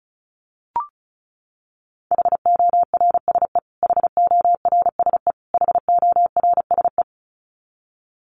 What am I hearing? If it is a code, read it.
Morse code, heard as a keyed tone, about 34 wpm, 700 Hz. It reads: HORSE HORSE HORSE
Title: Top 500 Words Repeated 35wpm